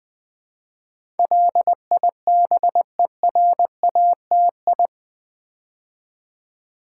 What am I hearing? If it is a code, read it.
Morse code, heard as a keyed tone, about 20 wpm, 690 Hz. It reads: LIBERATI
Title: Build A Word - 4 Letters At A Time - 10-Letter Words - 20wpm